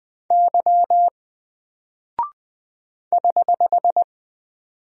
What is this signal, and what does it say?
Morse code, heard as a keyed tone, about 20 wpm, 700 Hz. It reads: Y <HH>
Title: Single Character (Letter:Number:Punctuation:Prosign) 20wpm